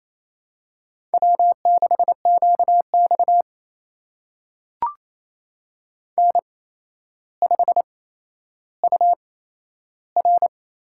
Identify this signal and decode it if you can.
Morse code, heard as a keyed tone, about 28 wpm, spaced as for 12 wpm, 700 Hz. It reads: W6QX N5UR
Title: USA Call Signs - 4 Characters - 8x Character Spacing - 28wpm